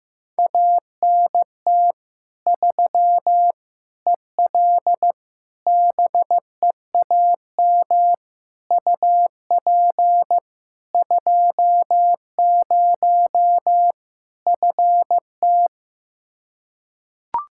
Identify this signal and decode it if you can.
Morse code, heard as a keyed tone, about 15 wpm, 700 Hz. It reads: ANT 3 EL BEAM UP 20 FT
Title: Practice QSO - Exchange 2 - Info - 15wpm